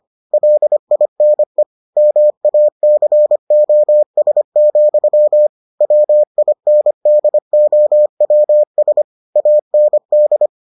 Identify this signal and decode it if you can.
Morse code, heard as a keyed tone, about 25 wpm, 600 Hz. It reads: LINE MACOS, WINDOWS AND